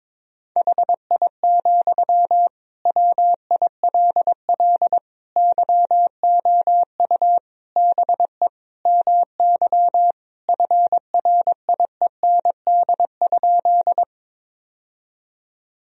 Morse code, heard as a keyed tone, about 22 wpm, 700 Hz. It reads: HI, WILL YOU BE MY FRIEND?
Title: Sentences from Top 1000 Words - Review Part 3 - 22wpm